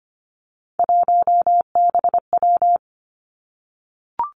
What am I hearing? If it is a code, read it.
Morse code, heard as a keyed tone, about 25 wpm, 700 Hz. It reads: 1BW
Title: Three Challenging Letter-Number Combinations 25wpm